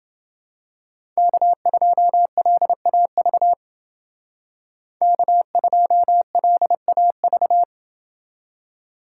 Morse code, heard as a keyed tone, about 29 wpm, 700 Hz. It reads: K2LAV K2LAV